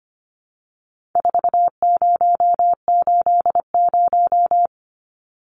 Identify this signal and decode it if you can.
Morse code, heard as a keyed tone, about 25 wpm, 700 Hz. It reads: 4080
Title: Numbers - 4-Digit 25wpm